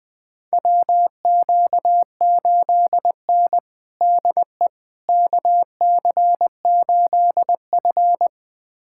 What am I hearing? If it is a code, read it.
Morse code, heard as a keyed tone, about 20 wpm, 700 Hz. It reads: WQ8N DE KC8F
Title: USA Callsign DE Callsign - 4 Characters - 20wpm